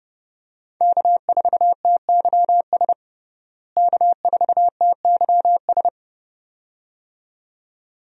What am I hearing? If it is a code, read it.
Morse code, heard as a keyed tone, about 32 wpm, 700 Hz. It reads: K4TYS K4TYS
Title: USA Call Signs - 5 Characters - Repeated 2x - 30wpm